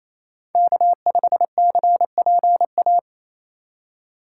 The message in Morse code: K5CPA

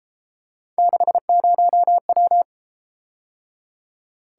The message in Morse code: 60W

